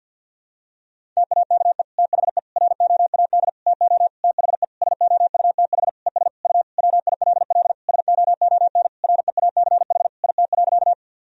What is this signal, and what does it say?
Morse code, heard as a keyed tone, about 50 wpm, 700 Hz. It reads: TAKE THE ROAD TO THE SOUTH ES U WILL SOON REACH IT.
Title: Sentences from Top 400 Words and Abbreviations 50wpm